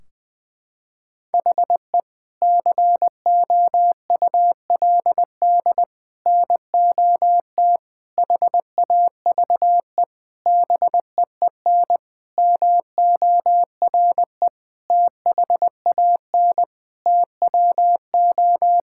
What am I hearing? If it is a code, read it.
Morse code, heard as a keyed tone, about 20 wpm, 700 Hz. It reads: HE COULD NOT HAVE BEEN MORE THAN TWO
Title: Sentences from Top 1600 Words 20wpm